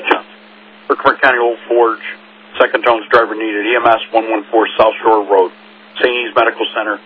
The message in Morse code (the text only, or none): none